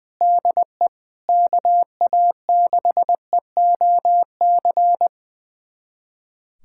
Morse code, heard as a keyed tone, about 20 wpm, 700 Hz. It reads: DE KA6EOC
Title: USA Callsign DE Callsign 20wpm